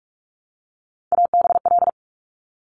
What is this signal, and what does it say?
Morse code, heard as a keyed tone, about 46 wpm, 700 Hz. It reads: ABL
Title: Top Three Letter Combinations in Words 45wpm